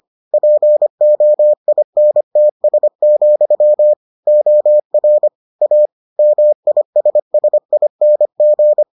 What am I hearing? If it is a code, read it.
Morse code, heard as a keyed tone, about 25 wpm, 600 Hz. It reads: POINTS, OR A MISSING